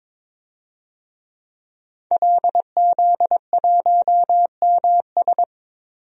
Morse code, heard as a keyed tone, about 22 wpm, 700 Hz. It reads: LZ1MS
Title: DXCC-100 Call Signs - 5 Characters - 22wpm